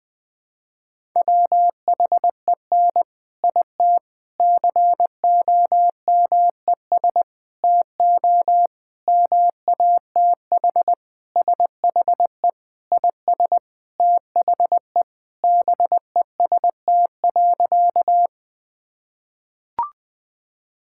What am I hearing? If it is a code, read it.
Morse code, heard as a keyed tone, about 21 wpm, 700 Hz. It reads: WHEN IT COMES TO MATH SHE IS THE BEST.
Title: Sentences with 4-Word Phrases - 20wpm